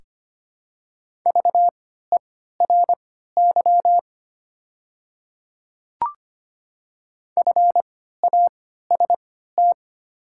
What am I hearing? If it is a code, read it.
Morse code, heard as a keyed tone, about 25 wpm, spaced as for 14 wpm, 700 Hz. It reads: VERY FAST
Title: 3x Character Spacing - Top 500 Words - 25wpm